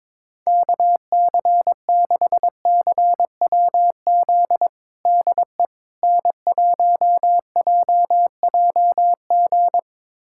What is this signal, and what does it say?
Morse code, heard as a keyed tone, about 22 wpm, 700 Hz. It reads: KC6CWZ DE N1JJG
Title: USA Callsign DE Callsign 22wpm